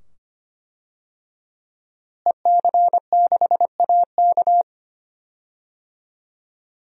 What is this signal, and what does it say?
Morse code, heard as a keyed tone, about 25 wpm, 700 Hz. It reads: EC6AK
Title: DXCC-100 Call Signs - 5 Characters - Repeated 2x - 25wpm